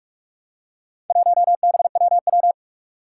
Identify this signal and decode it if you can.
Morse code, heard as a keyed tone, about 44 wpm, 700 Hz. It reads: 1BWW
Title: Four Challenging Letter-Number Combinations 45wpm